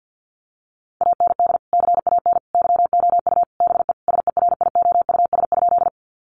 Morse code, heard as a keyed tone, about 48 wpm, 700 Hz. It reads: AND CAN YOU BE SERIOUS?
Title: Sentences from Top 1100 Words 50wpm